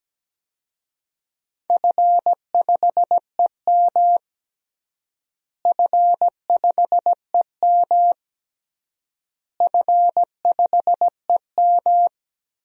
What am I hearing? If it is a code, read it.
Morse code, heard as a keyed tone, about 17 wpm, 700 Hz. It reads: F5EM F5EM F5EM
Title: DXCC-100 Call Signs - 4 Characters - Repeated 3x - 17wpm